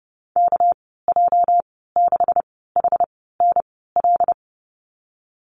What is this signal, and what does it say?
Morse code, heard as a keyed tone, about 30 wpm, spaced as for 17 wpm, 700 Hz. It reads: KJ6HNL
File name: Lesson 37g - Letter x - Call Signs 30wpm